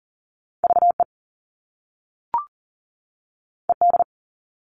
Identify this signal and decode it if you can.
Morse code, heard as a keyed tone, about 41 wpm, 700 Hz. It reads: VE ED
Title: All Cut Numbers - 2-Digit 40wpm